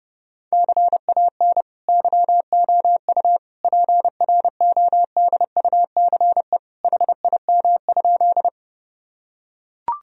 Morse code, heard as a keyed tone, about 30 wpm, 700 Hz. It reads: CAN YOU PRODUCE HIM?
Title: Sentences from Top 1000 Words - Review Part 3 - 30wpm